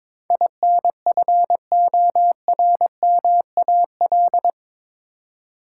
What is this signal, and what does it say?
Morse code, heard as a keyed tone, about 22 wpm, 700 Hz. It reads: INFORMAL
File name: practice / 22wpm / Eight Letter Words - 22wpm